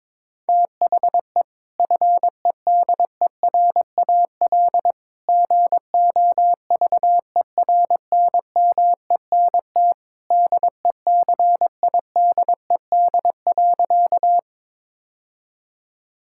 Morse code, heard as a keyed tone, about 22 wpm, 700 Hz. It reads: THE FEDERAL GOVERNMENT DECIDED.
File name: practice / 22wpm / Sentences with 3-Word Phrases - 22wpm